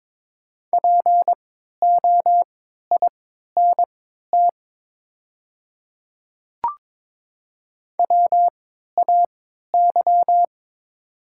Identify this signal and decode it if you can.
Morse code, heard as a keyed tone, about 22 wpm, spaced as for 12 wpm, 700 Hz. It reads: POINT WAY
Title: 3x Character Spacing - Top 300 Words - 22wpm